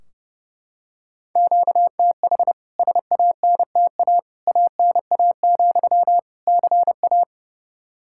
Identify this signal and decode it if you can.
Morse code, heard as a keyed tone, about 30 wpm, 700 Hz. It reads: QTH SANTA ANA, CA
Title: K1USN SST QSO Practice 30wpm